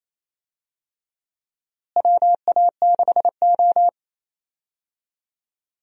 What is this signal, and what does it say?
Morse code, heard as a keyed tone, about 28 wpm, 700 Hz. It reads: WA6O
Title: USA Call Signs - 4 Characters - Repeated 2x - 28wpm